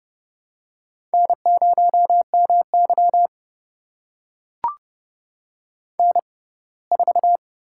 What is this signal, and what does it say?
Morse code, heard as a keyed tone, about 30 wpm, spaced as for 20 wpm, 700 Hz. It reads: N0MY N4
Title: USA Call Signs - 4 Characters - 6x Character Spacing - 30wpm